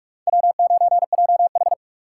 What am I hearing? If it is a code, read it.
Morse code, heard as a keyed tone, about 41 wpm, 700 Hz. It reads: W9JH